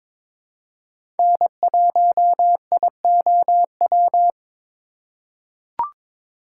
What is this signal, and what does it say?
Morse code, heard as a keyed tone, about 22 wpm, 700 Hz. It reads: N1IOW
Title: USA Call Signs - 5 Characters - 3x Character Spacing - 22wpm